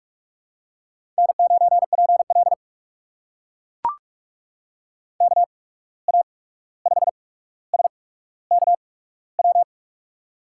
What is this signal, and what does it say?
Morse code, heard as a keyed tone, about 44 wpm, spaced as for 17 wpm, 700 Hz. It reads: N9PL KA5SKW
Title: USA Call Signs - 8x Character Spacing - 45wpm